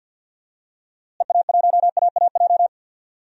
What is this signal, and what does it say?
Morse code, heard as a keyed tone, about 50 wpm, 700 Hz. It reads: EA1AAJ
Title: DXCC-100 Call Signs - 6 Characters - Repeated 2x - 50wpm